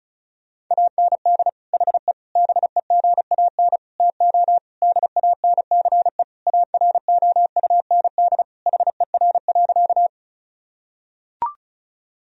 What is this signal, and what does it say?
Morse code, heard as a keyed tone, about 36 wpm, 700 Hz. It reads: AND HE BEGAN TO DANCE AROUND HER.
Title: Sentences from Top 1000 Words - Review Part 4 - 35wpm